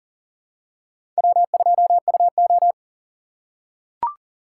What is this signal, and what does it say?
Morse code, heard as a keyed tone, about 40 wpm, 700 Hz. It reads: W2UO
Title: USA Call Signs - 4 Characters - 2x Character Spacing - 40wpm